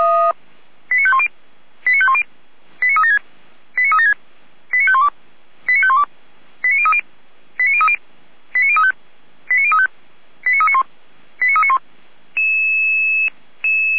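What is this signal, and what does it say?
Five-tone selective call (ZVEI), heard as quick runs of five tones, tones of about 80 ms. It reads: 89510 89510 89376 89376 89512 89512 89030 89030 89035 89035 89391 89391
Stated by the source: ZVEI five-tone call, code 89187, received by SDR.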